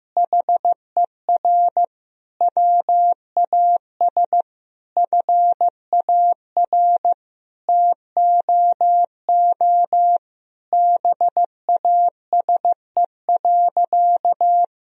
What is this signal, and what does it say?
Morse code, heard as a keyed tone, about 15 wpm, 700 Hz. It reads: HER WAS FAR TOO BASE.
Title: Sentences from Top 1000 Words - Review Part 1 - 15wpm